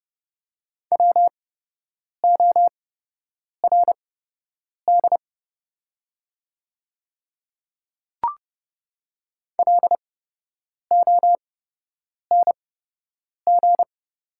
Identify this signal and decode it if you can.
Morse code, heard as a keyed tone, about 30 wpm, spaced as for 8 wpm, 700 Hz. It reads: WORD LONG
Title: 8x Character Spacing - Top 200 Words - 30wpm